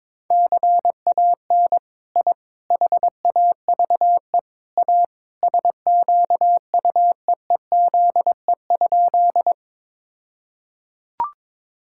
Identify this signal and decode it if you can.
Morse code, heard as a keyed tone, about 22 wpm, 700 Hz. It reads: CAN I HAVE A SQUEEZE?
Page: morse-code-ninja-practice-22wpm